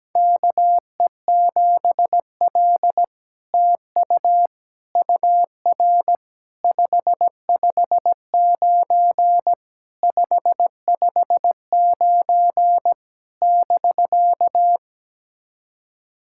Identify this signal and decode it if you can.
Morse code, heard as a keyed tone, about 17 wpm, 700 Hz. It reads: KE7L TU UR 559 559 <BK>
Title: SOTA QSO Practice 17wpm